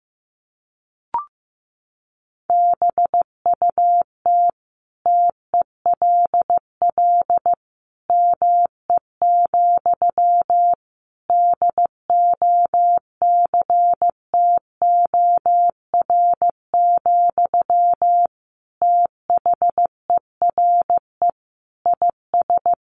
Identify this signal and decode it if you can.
Morse code, heard as a keyed tone, about 14 wpm, 700 Hz. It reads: BUT TELL ME, DOCTOR, THERE IS
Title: Sentences from Top 1500 Words 15wpm